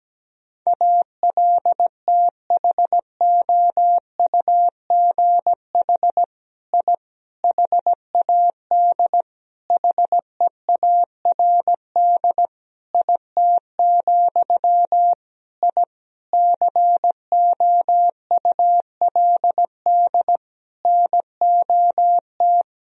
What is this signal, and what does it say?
Morse code, heard as a keyed tone, about 17 wpm, 700 Hz. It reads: ALTHOUGH I HAD HEARD IT, I COULD NOT
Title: Sentences from Top 1000 Words - Review Part 3 - 17wpm